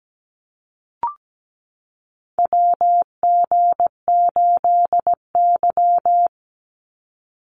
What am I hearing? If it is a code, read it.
Morse code, heard as a keyed tone, about 17 wpm, 700 Hz. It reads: WG8Y
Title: NA SOTA Chaser Callsigns 17wpm